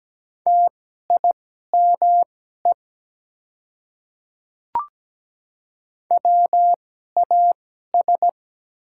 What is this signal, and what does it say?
Morse code, heard as a keyed tone, about 17 wpm, spaced as for 12 wpm, 700 Hz. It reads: TIME WAS
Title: 2x Character Spacing - Top 200 Words - 17wpm